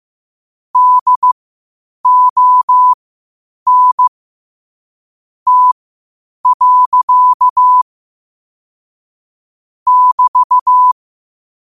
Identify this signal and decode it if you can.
Morse code, heard as a keyed tone, about 15 wpm, spaced as for 8 wpm, 1000 Hz. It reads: DON T. <BT>
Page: New quotes every day in morse code at 8 Words per minute.